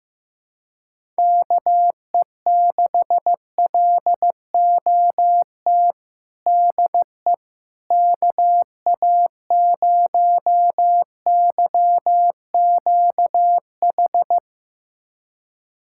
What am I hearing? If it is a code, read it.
Morse code, heard as a keyed tone, about 15 wpm, 700 Hz. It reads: KE6LOT DE KA0YQH
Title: USA Callsign DE Callsign 15wpm